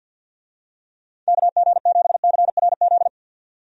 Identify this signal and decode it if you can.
Morse code, heard as a keyed tone, about 50 wpm, 700 Hz. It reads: KG7CRZ
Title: USA Call Signs - 6 Characters - 50wpm